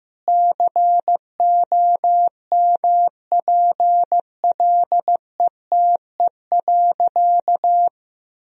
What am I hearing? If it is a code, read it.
Morse code, heard as a keyed tone, about 16 wpm, 700 Hz. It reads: COMPLETE.
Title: Sentences from Top 1000 Words - Review Part 2 - 15wpm